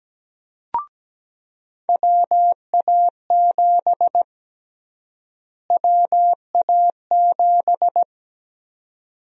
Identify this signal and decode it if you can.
Morse code, heard as a keyed tone, about 17 wpm, 700 Hz. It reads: WA7 WA7
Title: USA Call Sign Prefixes - Repeated 2x - 17wpm